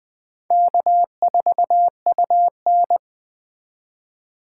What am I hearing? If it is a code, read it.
Morse code, heard as a keyed tone, about 20 wpm, 700 Hz. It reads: K4UN
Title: Speed Racing - USA 1x2 Call Signs 17wpm